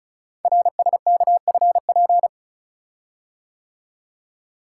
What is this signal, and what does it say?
Morse code, heard as a keyed tone, about 33 wpm, 700 Hz. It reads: RSKFP
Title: Code Groups - 5 Letters - 35wpm